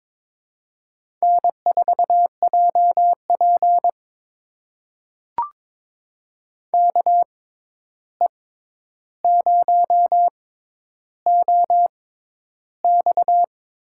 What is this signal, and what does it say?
Morse code, heard as a keyed tone, about 22 wpm, spaced as for 11 wpm, 700 Hz. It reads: N4JP KE0OX
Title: USA Call Signs - 6x Character Spacing - 22wpm